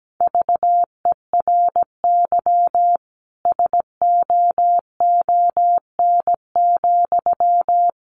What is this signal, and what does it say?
Morse code, heard as a keyed tone, about 17 wpm, 700 Hz. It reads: VERY SOON,
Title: Sentences from Top 1400 Words 17wpm